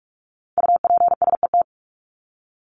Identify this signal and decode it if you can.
Morse code, heard as a keyed tone, about 44 wpm, 700 Hz. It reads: UPSET
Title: Top 751-1000 Words 45wpm